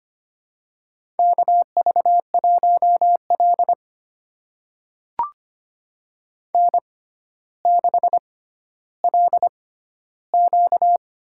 Morse code, heard as a keyed tone, about 25 wpm, spaced as for 13 wpm, 700 Hz. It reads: KV1L N6LQ
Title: USA Call Signs - 4 Characters - 6x Character Spacing - 25wpm